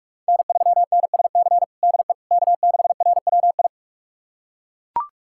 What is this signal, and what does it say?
Morse code, heard as a keyed tone, about 41 wpm, 700 Hz. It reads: N3NSC DE K6RWI
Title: USA Callsign DE Callsign - 5 Characters - 45wpm